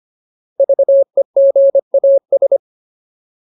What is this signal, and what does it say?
Morse code, heard as a keyed tone, about 25 wpm, 550 Hz. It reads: VEGAS
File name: Ultimate Word List / 25WPM Ultimate